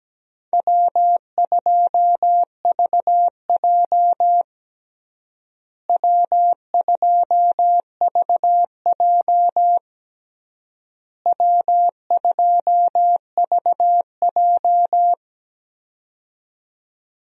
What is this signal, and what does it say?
Morse code, heard as a keyed tone, about 17 wpm, 700 Hz. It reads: W2VJ W2VJ W2VJ
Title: USA Call Signs - 4 Characters - Repeated 3x - 17wpm